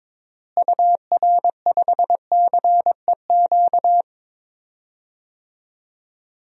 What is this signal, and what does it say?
Morse code, heard as a keyed tone, about 22 wpm, 700 Hz. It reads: UR5CEQ
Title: DXCC-100 Call Signs - 6 Characters - Repeated 2x - 22wpm